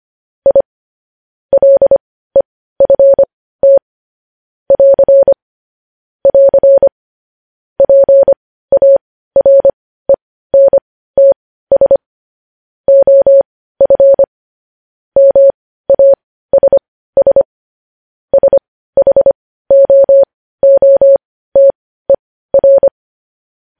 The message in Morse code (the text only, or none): I LEFT <AR> <AR> PARENTS OF MASS SHOOTER